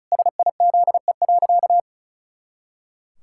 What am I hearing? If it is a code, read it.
Morse code, heard as a keyed tone, about 34 wpm, 700 Hz. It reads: SIZE.
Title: Pangram Challenge v2 - 35wpm